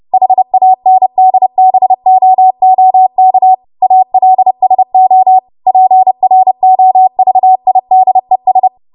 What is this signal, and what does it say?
Morse code, heard as a keyed tone, about 30 wpm, 750 Hz. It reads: HANDBOOK ALSO PROVIDES
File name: Morse / 30WPM